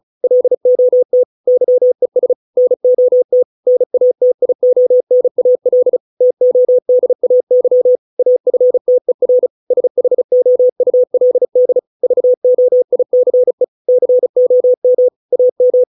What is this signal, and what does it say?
Morse code, heard as a keyed tone, about 34 wpm, 500 Hz. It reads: LOT YES NOT NATIONAL TODAY AFTER SHOULD VOICE COM AM